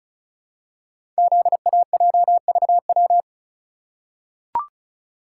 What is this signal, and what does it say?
Morse code, heard as a keyed tone, about 33 wpm, 700 Hz. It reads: ZAJVW
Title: Code Groups - 5 Letters - 35wpm